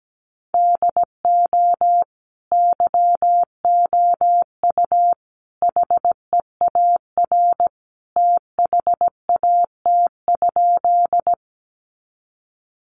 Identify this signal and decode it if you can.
Morse code, heard as a keyed tone, about 17 wpm, 700 Hz. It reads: DO YOU HEAR THAT?
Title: Sentences from Top 300 Words 17wpm